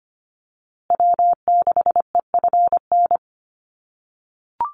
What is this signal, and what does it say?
Morse code, heard as a keyed tone, about 25 wpm, 700 Hz. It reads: W6EFN
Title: USA Call Signs Repeated 3x 25wpm